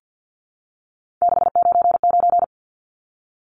Morse code, heard as a keyed tone, about 50 wpm, 700 Hz. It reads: 699
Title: Numbers - 3-Digit 50wpm